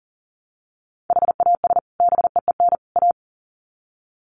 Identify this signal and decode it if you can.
Morse code, heard as a keyed tone, about 40 wpm, 700 Hz. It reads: HAS BEEN A
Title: Sets of 3 Words 40wpm